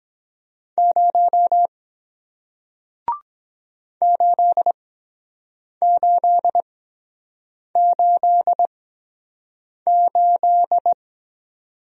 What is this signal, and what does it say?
Morse code, heard as a keyed tone, about 23 wpm, spaced as for 7 wpm, 700 Hz. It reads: 0 8888